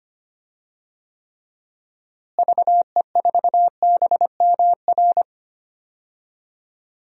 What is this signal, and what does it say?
Morse code, heard as a keyed tone, about 25 wpm, 700 Hz. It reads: VE4BMR
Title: DXCC-100 Call Signs - 6 Characters - Repeated 3x - 25wpm